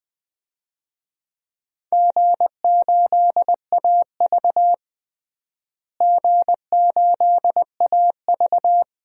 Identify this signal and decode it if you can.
Morse code, heard as a keyed tone, about 20 wpm, 700 Hz. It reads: G8AV G8AV
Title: DXCC-100 Call Signs - 4 Characters - Repeated 2x - 20wpm